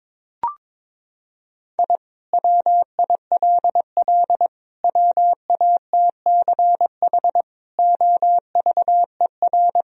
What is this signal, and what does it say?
Morse code, heard as a keyed tone, about 22 wpm, 700 Hz. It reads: I WILL WATCH OVER
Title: Sentences from Top 1000 Words - Review Part 1 - 22wpm